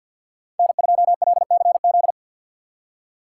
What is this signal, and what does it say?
Morse code, heard as a keyed tone, about 49 wpm, 700 Hz. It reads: N2LCZ
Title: Speed Racing - USA Call Signs 50wpm